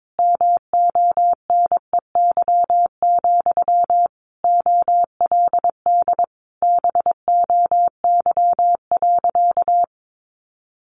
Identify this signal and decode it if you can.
Morse code, heard as a keyed tone, about 22 wpm, 700 Hz. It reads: MONEY, OLD BOY.
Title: Sentences from Top 1500 Words 22wpm